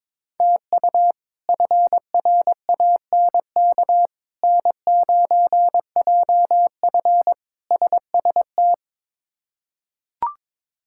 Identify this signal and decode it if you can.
Morse code, heard as a keyed tone, about 22 wpm, 700 Hz. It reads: TU FRANK N9JF SST
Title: K1USN SST QSO Practice - 22wpm